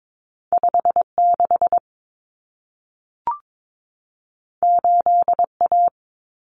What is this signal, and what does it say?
Morse code, heard as a keyed tone, about 22 wpm, 700 Hz. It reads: 56 8A
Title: Common Cut Numbers - 2-Digit 22wpm